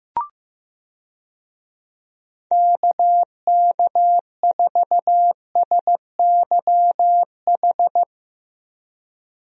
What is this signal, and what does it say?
Morse code, heard as a keyed tone, about 15 wpm, 700 Hz. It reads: KK4SYH
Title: USA Call Signs - 6 Characters - Repeated 3x - 15wpm